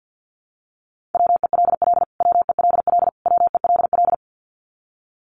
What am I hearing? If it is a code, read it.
Morse code, heard as a keyed tone, about 49 wpm, 700 Hz. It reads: WELL WELL WELL
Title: Top 500 Words Repeated 50wpm